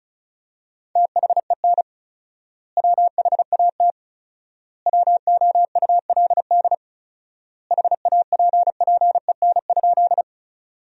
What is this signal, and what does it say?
Morse code, heard as a keyed tone, about 34 wpm, 700 Hz. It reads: THEN WHAT WOULD HAPPEN?
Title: 4x Word-Spacing - Sentences from Top 400 Words - 35wpm